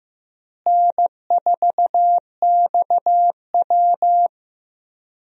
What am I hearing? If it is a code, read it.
Morse code, heard as a keyed tone, about 15 wpm, 700 Hz. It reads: N4XW
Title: USA Call Signs - 4 Characters - Repeated 2x - 15wpm